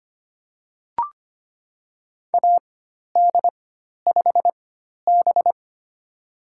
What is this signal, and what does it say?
Morse code, heard as a keyed tone, about 25 wpm, spaced as for 12 wpm, 700 Hz. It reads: AD5B
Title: USA Call Signs - 4 Characters - 4x Character Spacing - 25wpm